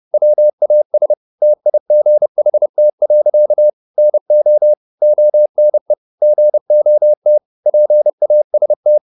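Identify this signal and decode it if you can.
Morse code, heard as a keyed tone, about 30 wpm, 600 Hz. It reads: WAS TIGHT. NO ONE GOT PAST
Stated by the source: Morse code at 29 wpm, sending Ken Follett's novel Never in English, from CW med Gnister.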